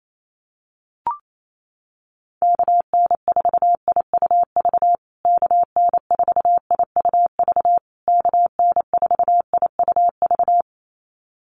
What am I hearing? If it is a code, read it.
Morse code, heard as a keyed tone, about 28 wpm, 700 Hz. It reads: KN4IUV KN4IUV KN4IUV
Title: USA Call Signs Repeated 3x 28wpm